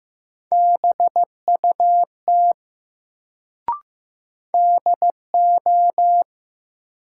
Morse code, heard as a keyed tone, about 15 wpm, 700 Hz. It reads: BUT DO
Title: Sentences from Top 100 Words - New Word Drill - 15wpm